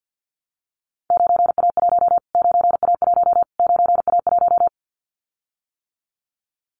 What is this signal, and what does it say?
Morse code, heard as a keyed tone, about 52 wpm, 700 Hz. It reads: 9A1 9A1 9A1
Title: Callsign Prefixes for Top 25 DXCC - Repeated 3x - 50wpm